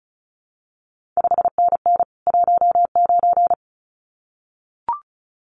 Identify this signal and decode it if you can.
Morse code, heard as a keyed tone, about 36 wpm, 700 Hz. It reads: 5NN 19
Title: CQ WW DX Contest Exchange Encore - 35wpm